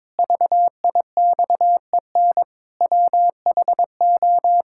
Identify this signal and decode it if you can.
Morse code, heard as a keyed tone, about 22 wpm, 700 Hz. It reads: VIXEN WHO
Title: Pangram Challenge v2 - 22wpm